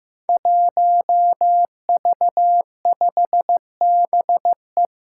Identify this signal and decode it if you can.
Morse code, heard as a keyed tone, about 15 wpm, 700 Hz. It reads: 1V5BE
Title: Five Challenging Letter-Number Combinations 15wpm